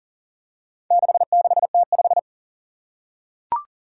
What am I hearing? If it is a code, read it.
Morse code, heard as a keyed tone, about 38 wpm, 700 Hz. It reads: 66T5
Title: Four Challenging Letter-Number Combinations 40wpm